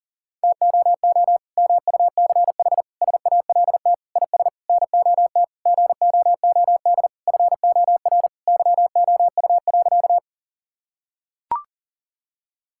Morse code, heard as a keyed tone, about 40 wpm, 700 Hz. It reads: TOO MUCH SALT IS NOT GOOD FOR YOU.
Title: Sentences from Top 1000 Words - Review Part 5 - 40wpm